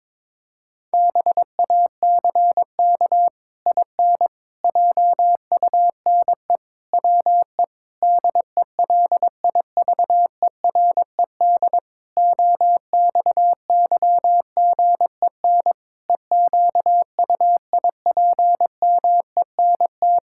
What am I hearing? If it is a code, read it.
Morse code, heard as a keyed tone, about 22 wpm, 700 Hz. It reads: BACK IN JUNE WE DELIVERED OXYGEN EQUIPMENT
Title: Pangram Challenge v2 - 22wpm